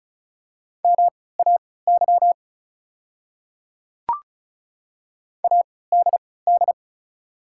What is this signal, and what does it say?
Morse code, heard as a keyed tone, about 36 wpm, spaced as for 20 wpm, 690 Hz. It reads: MAY ADD
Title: 3x Character Spacing - Top 300 Words - 35wpm